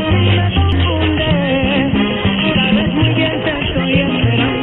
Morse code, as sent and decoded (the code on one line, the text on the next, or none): none